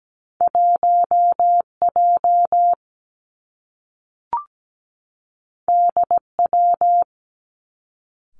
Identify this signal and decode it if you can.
Morse code, heard as a keyed tone, about 17 wpm, 700 Hz. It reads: 1 J D W
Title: Two Challenging Letter-Number Combinations 17wpm